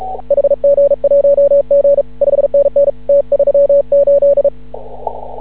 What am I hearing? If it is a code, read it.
Morse code, heard as a keyed tone, about 36 wpm, 560 Hz. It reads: HG1G 5NN T38
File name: CW / ha6oi 2